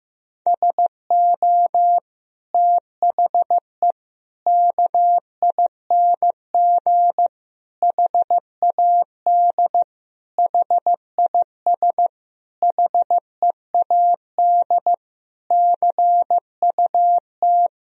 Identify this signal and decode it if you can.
Morse code, heard as a keyed tone, about 15 wpm, 700 Hz. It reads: SO THE KING HAD HIS HEAD CUT
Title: Sentences from Top 1000 Words - Review Part 1 - 15wpm